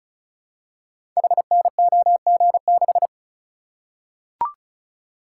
Morse code, heard as a keyed tone, about 36 wpm, 700 Hz. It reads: HNOG6
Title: Code Groups - 5 Letter:Numbers - 35wpm